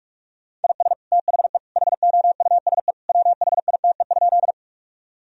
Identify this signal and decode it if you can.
Morse code, heard as a keyed tone, about 46 wpm, 700 Hz. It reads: IS THE HOUSE WHITE?
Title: Sentences from Top 1000 Words - Review Part 1 - 45wpm